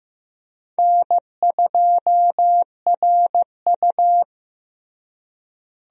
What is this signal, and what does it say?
Morse code, heard as a keyed tone, about 15 wpm, 700 Hz. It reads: N2RU
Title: USA Call Signs - 4 Characters - Repeated 3x - 15wpm